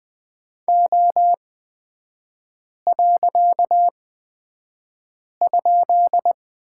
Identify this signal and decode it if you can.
Morse code, heard as a keyed tone, about 20 wpm, spaced as for 5 wpm, 690 Hz. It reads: O.?
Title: 255 Lesson 41e - Period - Three Characters 20wpm